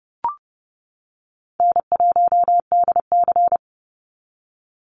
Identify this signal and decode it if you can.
Morse code, heard as a keyed tone, about 30 wpm, 700 Hz. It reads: N1DC
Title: Easy Call Signs - 30wpm